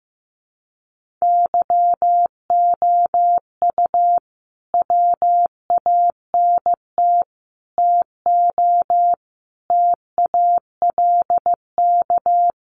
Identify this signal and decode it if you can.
Morse code, heard as a keyed tone, about 15 wpm, 700 Hz. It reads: YOU WANT TO TALK
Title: Sets of 5 Words Encore 15wpm